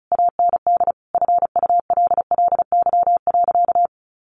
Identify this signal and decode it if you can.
Morse code, heard as a keyed tone, about 34 wpm, 700 Hz. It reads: AND FULLY.
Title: Sentences from Top 1300 Words 35wpm